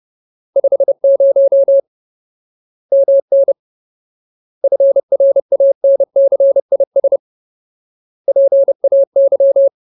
Morse code, heard as a keyed tone, about 30 wpm, 550 Hz. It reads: 50 MN FRANCIS PAY
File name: Ultimate Word List / 30WPM Ultimate